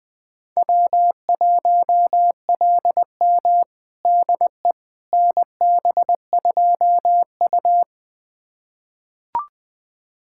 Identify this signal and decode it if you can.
Morse code, heard as a keyed tone, about 20 wpm, 700 Hz. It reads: W1LM DE NB2U
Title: USA Callsign DE Callsign - 4 Characters - 20wpm